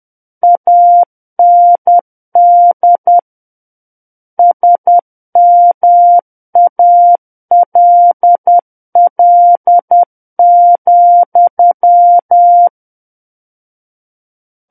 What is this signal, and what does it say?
Morse code, heard as a keyed tone, about 10 wpm, 700 Hz. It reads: AND SMALL,
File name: Book One / 01-Chapter 1 - 10 WPM